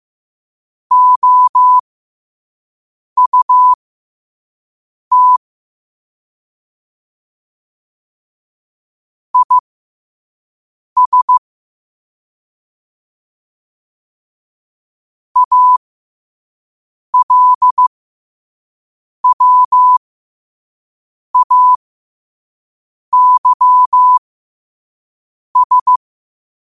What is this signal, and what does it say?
Morse code, heard as a keyed tone, about 15 wpm, spaced as for 5 wpm, 1000 Hz. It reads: OUT IS ALWAYS
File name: qotdfiles / 5wpm